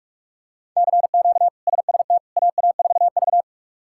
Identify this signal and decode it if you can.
Morse code, heard as a keyed tone, about 42 wpm, 700 Hz. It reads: CQ SST AA4V
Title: K1USN SST QSO Practice - 45wpm